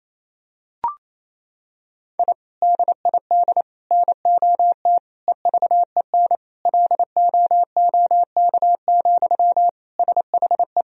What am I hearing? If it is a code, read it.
Morse code, heard as a keyed tone, about 28 wpm, 700 Hz. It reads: I DID NOT EVEN LOOK, SHE
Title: Sentences from Top 1000 Words - Review Part 1 - 28wpm